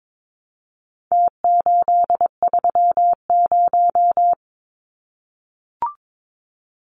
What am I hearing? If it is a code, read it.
Morse code, heard as a keyed tone, about 22 wpm, 700 Hz. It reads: T830
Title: Numbers - Contest Style - 4-Digit 22wpm